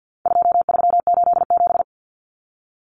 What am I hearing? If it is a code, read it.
Morse code, heard as a keyed tone, about 50 wpm, 700 Hz. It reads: 2387
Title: Common Cut Numbers - 4-Digit 50wpm